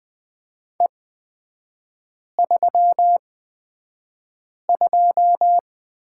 Morse code, heard as a keyed tone, about 20 wpm, spaced as for 5 wpm, 700 Hz. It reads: E32
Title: 224 Lesson 36e - BT - Three Characters 20wpm